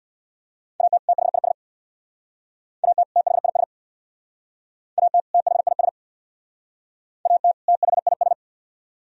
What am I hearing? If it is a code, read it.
Morse code, heard as a keyed tone, about 61 wpm, 700 Hz. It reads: AT THIS AT THIS AT THIS AT THIS
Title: Speed-Racing - Sets of 2 Words 50wpm